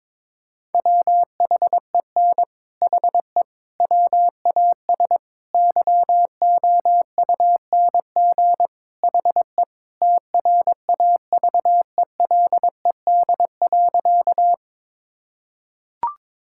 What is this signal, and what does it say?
Morse code, heard as a keyed tone, about 22 wpm, 700 Hz. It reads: WHEN HE WAS YOUNG HE TRAVELED.
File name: practice / 22wpm / Sentences with 3-Word Phrases - 22wpm